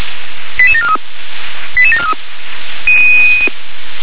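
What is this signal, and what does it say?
Five-tone selective call (ZVEI), heard as quick runs of five tones, tones of about 70 ms. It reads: 88743 88743
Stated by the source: ZVEI five-tone selective call received on SDR.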